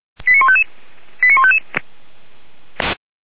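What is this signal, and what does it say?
Five-tone selective call (ZVEI), heard as quick runs of five tones, tones of about 70 ms. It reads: 89155 89155
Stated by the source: ZVEI five-tone selective call received on an SDR.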